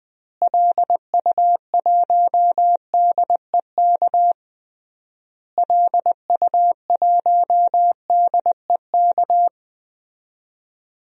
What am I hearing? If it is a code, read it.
Morse code, heard as a keyed tone, about 20 wpm, 700 Hz. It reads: LU1DEK LU1DEK
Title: DXCC-100 Call Signs - 6 Characters - Repeated 3x - 20wpm